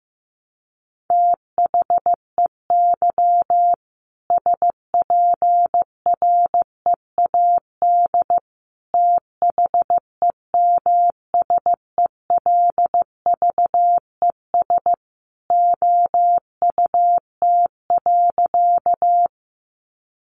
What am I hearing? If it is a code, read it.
Morse code, heard as a keyed tone, about 15 wpm, 700 Hz. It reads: THEY SPREAD THEMSELVES OUT.
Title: Sentences from Top 1200 Words 15wpm